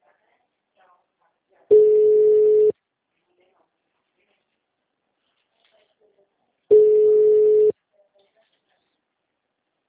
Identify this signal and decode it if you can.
Ringback tone (Europe, 425 Hz).
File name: out- -2002- - - .28